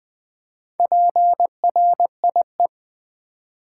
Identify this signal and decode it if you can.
Morse code, heard as a keyed tone, about 20 wpm, 690 Hz. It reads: PRIE
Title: Build A Word - 2 Letters At A Time - 6-Letter Words - 20wpm